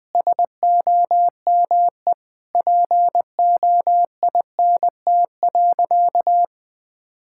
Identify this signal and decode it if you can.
Morse code, heard as a keyed tone, about 21 wpm, 700 Hz. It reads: SOME POINT.
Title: Sentences from Top 1000 Words - Review Part 5 - 20wpm